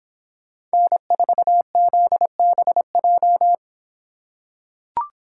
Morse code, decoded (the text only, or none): N4ZBJ